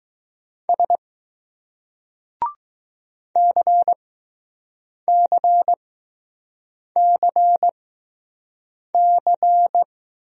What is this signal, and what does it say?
Morse code, heard as a keyed tone, about 20 wpm, spaced as for 6 wpm, 700 Hz. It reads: S CCCC